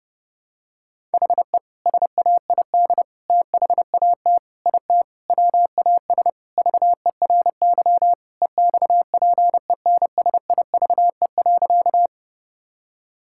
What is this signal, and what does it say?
Morse code, heard as a keyed tone, about 32 wpm, 700 Hz. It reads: HE SAID THAT IT WAS VERY EXPENSIVE.
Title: Sentences from Top 1000 Words - Review Part 6 - 30wpm